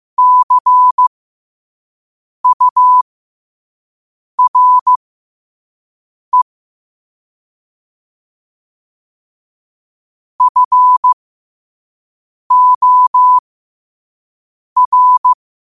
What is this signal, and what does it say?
Morse code, heard as a keyed tone, about 15 wpm, spaced as for 5 wpm, 1000 Hz. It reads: CURE FOR
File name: qotdfiles / 5wpm